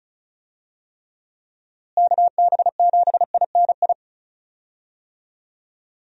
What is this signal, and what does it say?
Morse code, heard as a keyed tone, about 36 wpm, 700 Hz. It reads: KB7INI
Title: USA Call Signs - 6 Characters - Repeated 2x - 35wpm